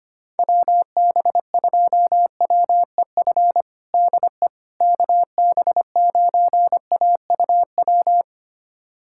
Morse code, heard as a keyed tone, about 25 wpm, 700 Hz. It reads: WB2WEF DE KB9AUW
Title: USA Callsign DE Callsign 25wpm